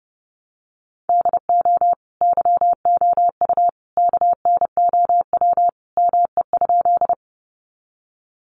Morse code, heard as a keyed tone, about 30 wpm, 700 Hz. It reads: DO YOU KNOW ME?
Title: Sentences from Top 200 Words 30wpm